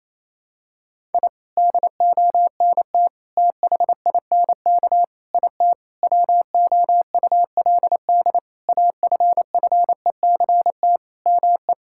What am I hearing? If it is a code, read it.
Morse code, heard as a keyed tone, about 28 wpm, 700 Hz. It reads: I DONT THINK IT WOULD AFFECT ME
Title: Sentences from Top 1000 Words - Review Part 6 - 28wpm